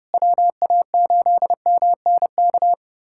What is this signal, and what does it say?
Morse code, heard as a keyed tone, about 32 wpm, 700 Hz. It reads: WA8MNK
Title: USA Callsign DE Callsign 30wpm